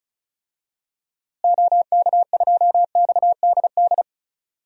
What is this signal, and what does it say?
Morse code, heard as a keyed tone, about 36 wpm, 700 Hz. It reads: OK2XDD
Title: DXCC-100 Call Signs - 6 Characters - Repeated 2x - 35wpm